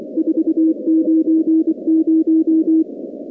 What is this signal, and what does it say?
Morse code, heard as a keyed tone, about 23 wpm, 320 Hz. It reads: 490
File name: contests / 2019-cw-wpx-kt5j-k5pi- -2330 0-46